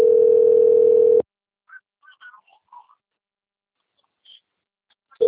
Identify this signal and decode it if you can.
Ringback tone (North America), 440 + 480 Hz.
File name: rg-1600-unknown- - - .39310